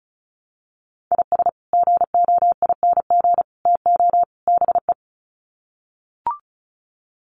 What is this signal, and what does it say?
Morse code, heard as a keyed tone, about 34 wpm, 700 Hz. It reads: IS GOING TO BE
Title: Sets of 4 Words 35wpm